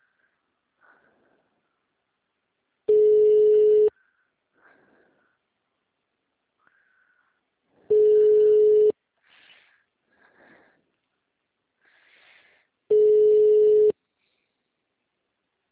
Ringback tone (Europe, 425 Hz).